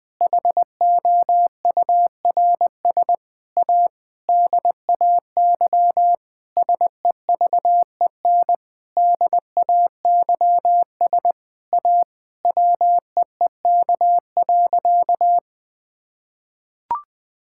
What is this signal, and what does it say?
Morse code, heard as a keyed tone, about 20 wpm, 700 Hz. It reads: HOURS A DAY SEVEN DAYS A WEEK.
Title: Sentences with 7-Word Phrases - Spoken Beginning - 20wpm